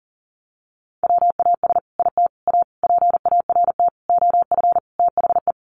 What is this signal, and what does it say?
Morse code, heard as a keyed tone, about 40 wpm, 700 Hz. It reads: WAS IT A PART OF THE
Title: Sentences from Top 1500 Words 40wpm